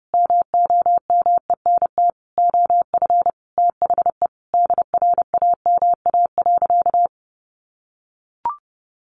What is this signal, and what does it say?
Morse code, heard as a keyed tone, about 30 wpm, 700 Hz. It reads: MOMENT OF THE DRAMA.
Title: Sentences from Top 1500 Words 30wpm